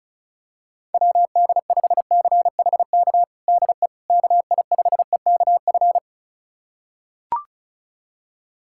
Morse code, heard as a keyed tone, about 36 wpm, 700 Hz. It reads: WD5CHK DE KI5EKF
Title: USA Callsign DE Callsign 35wpm